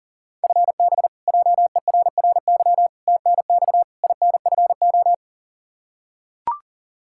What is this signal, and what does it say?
Morse code, heard as a keyed tone, about 40 wpm, 700 Hz. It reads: FB JERRY TNX INFO
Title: K1USN SST QSO Practice 40wpm